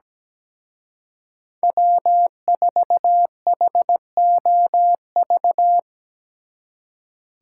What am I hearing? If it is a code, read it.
Morse code, heard as a keyed tone, about 19 wpm, 700 Hz. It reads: W4HOV
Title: Speed Racing - USA 1x3 Call Signs 17wpm